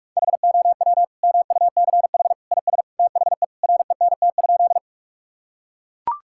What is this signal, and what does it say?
Morse code, heard as a keyed tone, about 42 wpm, 700 Hz. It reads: HOW MUCH IS THE RENT?
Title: Sentences from Top 1000 Words - Review Part 5 - 45wpm